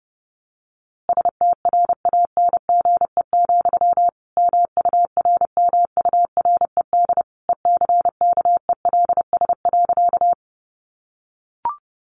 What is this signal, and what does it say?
Morse code, heard as a keyed tone, about 30 wpm, 700 Hz. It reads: STRANGE, MURMURED ECKELS.
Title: A Sound Of Thunder 30wpm